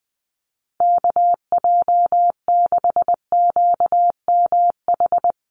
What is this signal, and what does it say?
Morse code, heard as a keyed tone, about 20 wpm, 700 Hz. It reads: KJ6QMH
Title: USA Call Signs Repeated 2x 20wpm